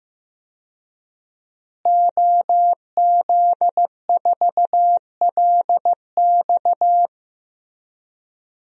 Morse code, heard as a keyed tone, about 16 wpm, 700 Hz. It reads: OZ4LX
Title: DXCC-100 Call Signs - 5 Characters - Repeated 2x - 15wpm